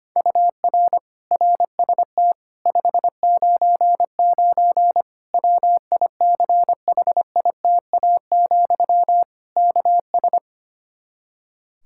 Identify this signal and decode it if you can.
Morse code, heard as a keyed tone, about 25 wpm, 700 Hz. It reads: UR RST 599 WICHITA, KS
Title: Practice QSO - Exchange 1 - Beginning - 25wpm